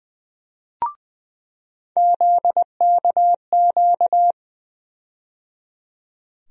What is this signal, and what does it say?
Morse code, heard as a keyed tone, about 21 wpm, 700 Hz. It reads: ZKQ
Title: USA Call Sign Postfixes - 3 Letters 20wpm